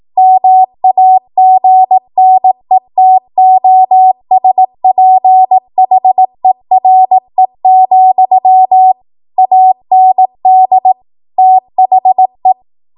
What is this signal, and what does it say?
Morse code, heard as a keyed tone, about 18 wpm, 750 Hz. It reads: MAGNETOSPHERE, AND THE
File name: Morse / 18WPM